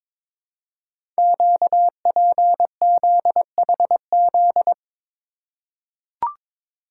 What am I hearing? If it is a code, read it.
Morse code, heard as a keyed tone, about 22 wpm, 700 Hz. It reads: QPZHZ
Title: Code Groups - 5 Letters - 22wpm